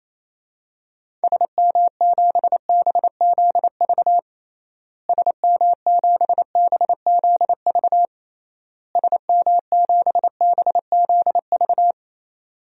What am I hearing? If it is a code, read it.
Morse code, heard as a keyed tone, about 28 wpm, 700 Hz. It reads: SM7BZV SM7BZV SM7BZV
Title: DXCC-100 Call Signs - 6 Characters - Repeated 3x - 28wpm